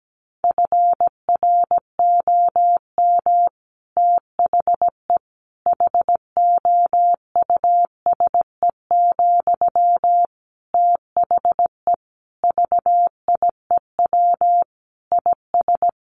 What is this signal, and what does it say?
Morse code, heard as a keyed tone, about 17 wpm, 700 Hz. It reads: FROM THE HOUSE, THE VIEW IS
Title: Sentences from Top 1100 Words 17wpm